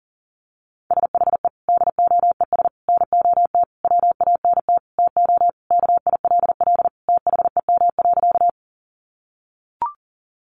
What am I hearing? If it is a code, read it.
Morse code, heard as a keyed tone, about 40 wpm, 700 Hz. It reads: SHE DOES NOT WANT TO KILL THEM.
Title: Additional Easy Phrases 40wpm